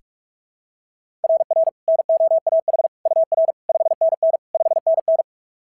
Morse code, heard as a keyed tone, about 43 wpm, 660 Hz. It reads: RR NOAH UR 5NN 5NN
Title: SOTA QSO Practice 45wpm